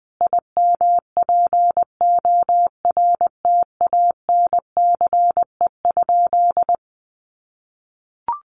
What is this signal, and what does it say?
Morse code, heard as a keyed tone, about 20 wpm, 700 Hz. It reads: IMPORTANCE?
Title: Sentences from Top 1200 Words 20wpm